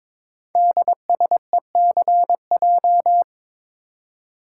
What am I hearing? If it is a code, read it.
Morse code, heard as a keyed tone, about 22 wpm, 700 Hz. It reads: DSECJ
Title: Code Groups - 5 Letters - 22wpm